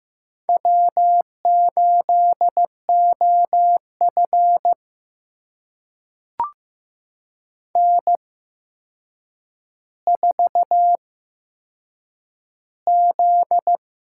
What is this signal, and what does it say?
Morse code, heard as a keyed tone, about 15 wpm, spaced as for 7 wpm, 700 Hz. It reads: W8OF N4Z
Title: USA Call Signs - 4 Characters - 8x Character Spacing - 15wpm